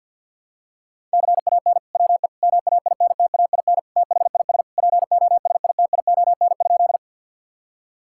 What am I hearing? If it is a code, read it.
Morse code, heard as a keyed tone, about 50 wpm, 700 Hz. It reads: CAN WE MAINTAIN THIS POSITION?
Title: Sentences from Top 1000 Words - Review Part 5 - 50wpm